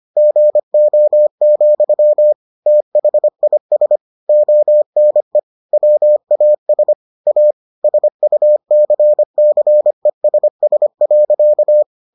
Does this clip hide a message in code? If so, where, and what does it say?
Yes: Morse code from 0.2 to 11.8 s, heard as a keyed tone, about 25 wpm, 600 Hz. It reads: GO, THIS ONE WAS A SUCCESS.